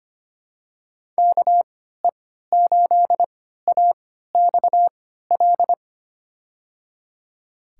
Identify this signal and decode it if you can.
Morse code, heard as a keyed tone, about 25 wpm, spaced as for 14 wpm, 700 Hz. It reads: KE8AXL
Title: USA Call Signs - 6 Characters - 3x Character Spacing - 25wpm